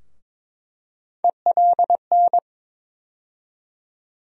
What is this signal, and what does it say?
Morse code, heard as a keyed tone, about 21 wpm, 700 Hz. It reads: ELN
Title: USA Call Sign Postfixes - 3 Letters 22wpm